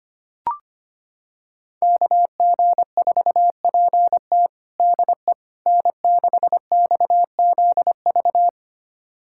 Morse code, heard as a keyed tone, about 25 wpm, 700 Hz. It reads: KG4PT DE N6XZV
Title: USA Callsign DE Callsign - 5 Characters - 25wpm